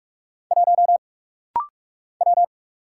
Morse code, heard as a keyed tone, about 46 wpm, 700 Hz. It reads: 1 W
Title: Single Letter-Number - Rapid-Fire 45wpm